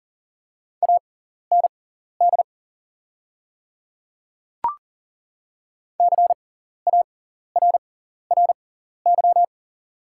Morse code, heard as a keyed tone, about 37 wpm, spaced as for 14 wpm, 690 Hz. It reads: AND CARRY
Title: 6x Character Spacing - Top 300 Words - 40wpm